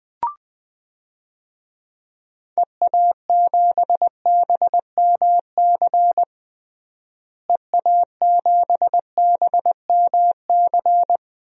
Morse code, heard as a keyed tone, about 20 wpm, 700 Hz. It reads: EA7BMC EA7BMC
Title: DXCC-100 Call Signs - 6 Characters - Repeated 2x - 20wpm